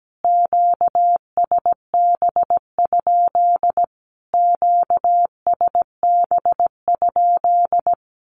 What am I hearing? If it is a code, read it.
Morse code, heard as a keyed tone, about 17 wpm, 700 Hz. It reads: QSB? QSB?
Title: Q-Codes Repeated 3x 17wpm